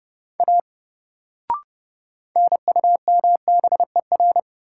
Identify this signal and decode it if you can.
Morse code, heard as a keyed tone, about 30 wpm, 700 Hz. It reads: A NUMBER
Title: Sentences from Top 100 Words - New Word Drill - 30wpm